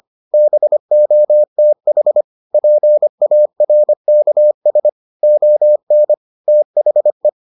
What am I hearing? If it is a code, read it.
Morse code, heard as a keyed tone, about 25 wpm, 600 Hz. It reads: BOTH PARKS ON THE